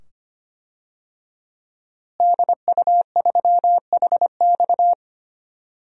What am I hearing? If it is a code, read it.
Morse code, heard as a keyed tone, about 25 wpm, 700 Hz. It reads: DU3HX
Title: DXCC-100 Call Signs - 5 Characters - Repeated 2x - 25wpm